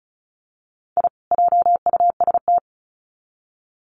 Morse code, heard as a keyed tone, about 34 wpm, 700 Hz. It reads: I JUST
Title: Sets of 2 Words 35wpm